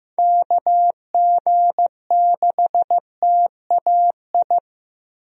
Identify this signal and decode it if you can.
Morse code, heard as a keyed tone, about 15 wpm, 700 Hz. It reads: KG6TAI
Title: USA Call Signs - 6 Characters - Repeated 2x - 15wpm